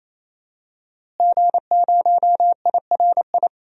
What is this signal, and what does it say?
Morse code, heard as a keyed tone, about 27 wpm, 700 Hz. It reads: G0IRI
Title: DXCC-100 Call Signs - 5 Characters - Repeated 3x - 28wpm